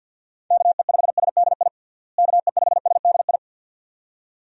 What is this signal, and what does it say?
Morse code, heard as a keyed tone, about 48 wpm, 700 Hz. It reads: KE5SDI KE5SDI
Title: USA Call Signs - 6 Characters - Repeated 2x - 50wpm